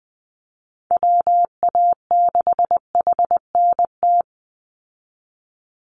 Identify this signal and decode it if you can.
Morse code, heard as a keyed tone, about 20 wpm, 700 Hz. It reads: WA6HNT
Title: USA Call Signs 20wpm